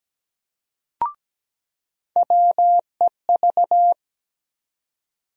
Morse code, heard as a keyed tone, about 17 wpm, 700 Hz. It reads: WEV
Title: USA Call Sign Postfixes - 3 Letters 17wpm